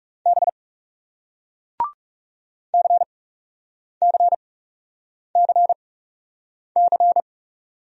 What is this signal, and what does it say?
Morse code, heard as a keyed tone, about 39 wpm, spaced as for 8 wpm, 700 Hz. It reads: B CCCC